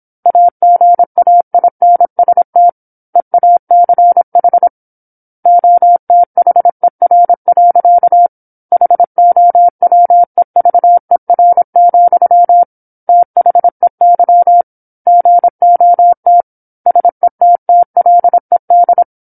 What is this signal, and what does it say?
Morse code, heard as a keyed tone, about 27 wpm, 700 Hz. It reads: AGAINST EACH OTHER. HOWEVER, THEY GOT SETTLED
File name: 08-Chapter-08 26 WPM